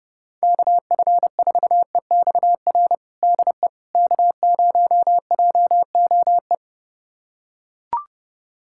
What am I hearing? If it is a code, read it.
Morse code, heard as a keyed tone, about 30 wpm, 700 Hz. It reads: KF4EXR DE K0JOE
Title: USA Callsign DE Callsign 30wpm